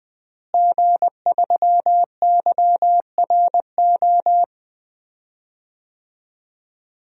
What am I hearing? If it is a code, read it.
Morse code, heard as a keyed tone, about 20 wpm, 700 Hz. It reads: G3YRO
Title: DXCC-100 Call Signs - 5 Characters - Repeated 3x - 20wpm